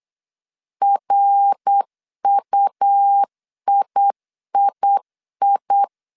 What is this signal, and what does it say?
Morse code, heard as a keyed tone, about 8 wpm, 800 Hz. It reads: RUIII
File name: CW / DET / 1 3 1500C S B 8wpm 4141